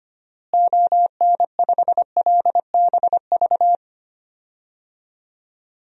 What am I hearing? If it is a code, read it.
Morse code, heard as a keyed tone, about 25 wpm, 700 Hz. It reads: ON5LBV
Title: DXCC-100 Call Signs - 6 Characters - Repeated 2x - 25wpm